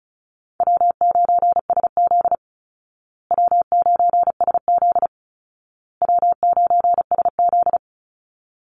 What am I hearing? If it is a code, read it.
Morse code, heard as a keyed tone, about 34 wpm, 700 Hz. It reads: W9SZ W9SZ W9SZ
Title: Easy Call Signs Repeated 3x - 35wpm